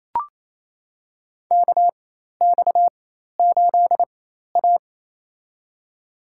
Morse code, heard as a keyed tone, about 28 wpm, spaced as for 13 wpm, 700 Hz. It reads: KX8A
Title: USA Call Signs - 4 Characters - 4x Character Spacing - 28wpm